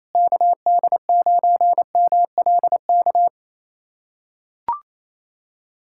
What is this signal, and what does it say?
Morse code, heard as a keyed tone, about 28 wpm, 700 Hz. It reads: KD9MLK
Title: USA Callsign DE Callsign 28wpm